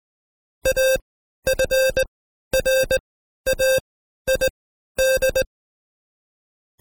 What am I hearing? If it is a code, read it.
Morse code, heard as a keyed tone, about 19 wpm, spaced as for 12 wpm, 530 Hz. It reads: AFRAID
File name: Code Quick / AudioDownld2 / Seg132-Stepper5 11WPM